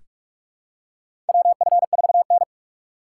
Morse code, heard as a keyed tone, about 46 wpm, 700 Hz. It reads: WF4N
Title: USA Call Signs - 4 Characters - 2x Character Spacing - 45wpm